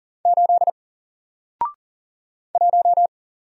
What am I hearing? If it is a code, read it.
Morse code, heard as a keyed tone, about 39 wpm, 700 Hz. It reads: 8 1
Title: Single Character (Letter:Number:Punctuation:Prosign) 40wpm